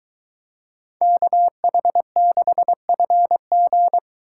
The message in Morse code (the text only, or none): KH6FG